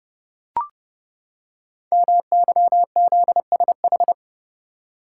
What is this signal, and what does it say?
Morse code, heard as a keyed tone, about 30 wpm, 700 Hz. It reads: MYZSH
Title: Code Groups - 5 Letter:Numbers - 30wpm